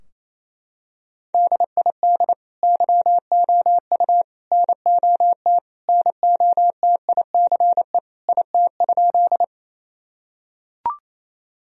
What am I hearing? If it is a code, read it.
Morse code, heard as a keyed tone, about 27 wpm, 690 Hz. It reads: DID YOU NOT NOTICE IT?
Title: Sentences from Top 1000 Words - Review Part 2 - 28wpm